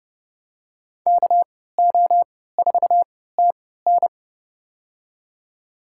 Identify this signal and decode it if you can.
Morse code, heard as a keyed tone, about 30 wpm, spaced as for 17 wpm, 700 Hz. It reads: KO4TN
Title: USA Call Signs - 5 Characters - 3x Character Spacing - 30wpm